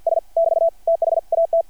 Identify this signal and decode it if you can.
Morse code, heard as a keyed tone, about 50 wpm, 660 Hz. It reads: S <BT> THAT
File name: archived / CRSnet